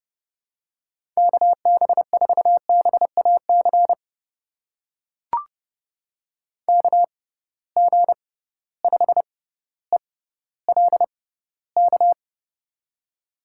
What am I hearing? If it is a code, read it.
Morse code, heard as a keyed tone, about 30 wpm, spaced as for 15 wpm, 700 Hz. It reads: KB4BAC KG5ELK
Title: USA Call Signs - 6 Characters - 6x Character Spacing - 30wpm